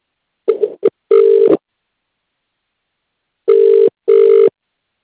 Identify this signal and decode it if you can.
Ringback tone (UK), 400 + 450 Hz.